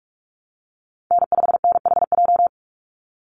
Morse code, heard as a keyed tone, about 44 wpm, 700 Hz. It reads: N5NHJ
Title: Call Signs Repeated 2x - 45wpm